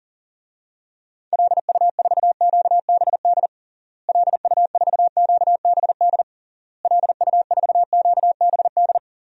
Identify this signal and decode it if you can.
Morse code, heard as a keyed tone, about 40 wpm, 700 Hz. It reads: LU4QBD LU4QBD LU4QBD
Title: DXCC-100 Call Signs - 6 Characters - Repeated 3x - 40wpm